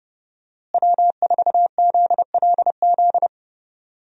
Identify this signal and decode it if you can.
Morse code, heard as a keyed tone, about 32 wpm, 700 Hz. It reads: W4ZLZ
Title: USA Call Signs - 5 Characters - Repeated 3x - 30wpm